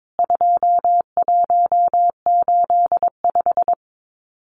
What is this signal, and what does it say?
Morse code, heard as a keyed tone, about 22 wpm, 700 Hz. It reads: 2185
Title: CWT Exchange 22wpm